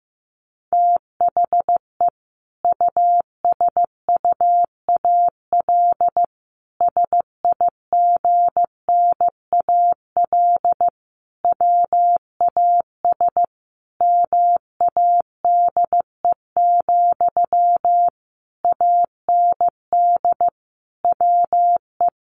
Morse code, heard as a keyed tone, about 15 wpm, 700 Hz. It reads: THE USUAL SIGNAL WAS MADE, AND WE
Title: Sentences from Top 1400 Words 15wpm